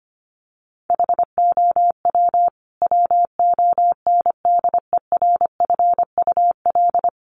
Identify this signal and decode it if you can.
Morse code, heard as a keyed tone, about 25 wpm, 700 Hz. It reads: HOW WONDERFUL
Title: Sentences from Top 1100 Words 25wpm